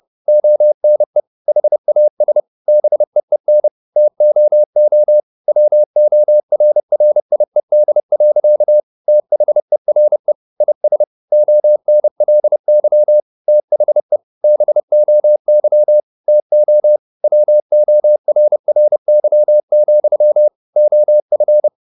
Morse code, heard as a keyed tone, about 30 wpm, 600 Hz. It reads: ONE HAS BEEN TOO WORRIED. THERE IS ONLY THE BOY TO WORRY, OF